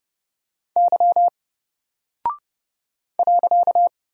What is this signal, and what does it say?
Morse code, heard as a keyed tone, about 30 wpm, 700 Hz. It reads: Y .
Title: Single Character (Letter:Number:Punctuation:Prosign) Alt 30wpm